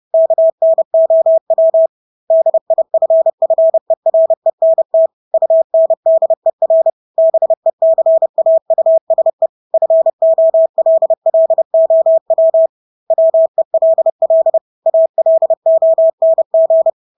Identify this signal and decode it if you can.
Morse code, heard as a keyed tone, about 30 wpm, 650 Hz. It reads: KNOW DIFFERENT UNDER BECAUSE FOLLOW WELL ALONG
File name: Engelske ord 30-30 wpm